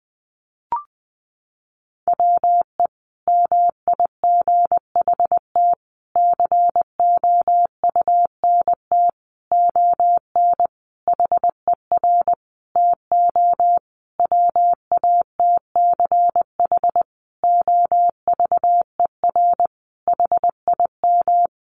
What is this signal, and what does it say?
Morse code, heard as a keyed tone, about 20 wpm, 700 Hz. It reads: WE MIGHT COUNT ON HER TO WATCH OVER HIM
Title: Sentences from Top 1500 Words 20wpm